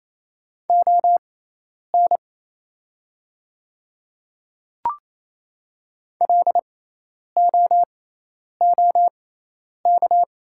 Morse code, heard as a keyed tone, about 28 wpm, spaced as for 10 wpm, 700 Hz. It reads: ON LOOK
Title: 6x Character Spacing - Top 100 Words - 28wpm